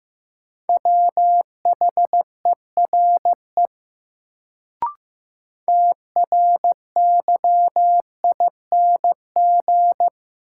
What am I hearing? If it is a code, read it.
Morse code, heard as a keyed tone, about 15 wpm, 700 Hz. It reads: WHERE TRYING
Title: Sets of 2 Words - New Word Drill - 15wpm